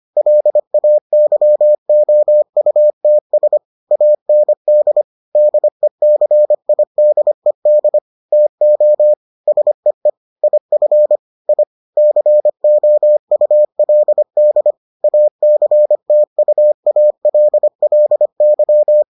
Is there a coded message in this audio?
Yes: Morse code from 0.2 to 19.0 s, heard as a keyed tone, about 25 wpm, 600 Hz. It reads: LAYOUTS AND DECIDED TO SEE IF I COULD ACTUALLY